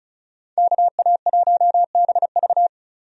Morse code, heard as a keyed tone, about 34 wpm, 700 Hz. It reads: KA1BV
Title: USA Call Signs - 5 Characters - 35wpm